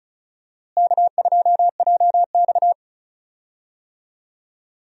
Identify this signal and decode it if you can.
Morse code, heard as a keyed tone, about 36 wpm, 700 Hz. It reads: K2JX
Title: USA Call Signs - 4 Characters - Repeated 2x - 35wpm